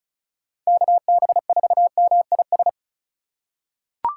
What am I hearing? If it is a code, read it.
Morse code, heard as a keyed tone, about 36 wpm, 690 Hz. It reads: KB4MIS
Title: USA Call Signs - 3x Character Spacing - 35wpm